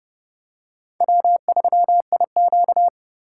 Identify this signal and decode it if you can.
Morse code, heard as a keyed tone, about 30 wpm, 700 Hz. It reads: W3IQ
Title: USA Call Signs - 4x Character Spacing - 30wpm